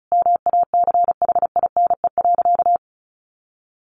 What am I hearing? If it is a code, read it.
Morse code, heard as a keyed tone, about 34 wpm, 700 Hz. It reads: MACHINE.
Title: A Sound Of Thunder 35wpm